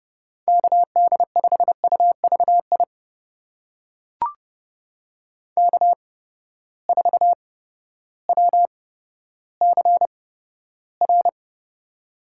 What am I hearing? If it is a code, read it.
Morse code, heard as a keyed tone, about 30 wpm, spaced as for 14 wpm, 700 Hz. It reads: KD5UVI K4WCR
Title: USA Call Signs - 8x Character Spacing - 30wpm